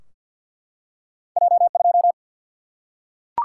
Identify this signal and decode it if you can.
Morse code, heard as a keyed tone, about 50 wpm, 700 Hz. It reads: J 2
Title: Single-Letter Single-Number 50wpm